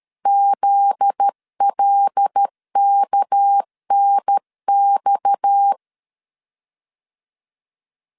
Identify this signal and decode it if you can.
Morse code, heard as a keyed tone, about 12 wpm, 800 Hz. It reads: ZLKNX
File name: LEZIONI / 7 GF9Z / 7 GF9Z 12wpm 2147